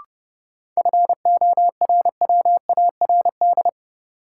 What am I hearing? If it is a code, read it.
Morse code, heard as a keyed tone, about 30 wpm, 700 Hz. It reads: FORWARD
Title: Sentences from Top 1300 Words - New Word Drill Alt - 30wpm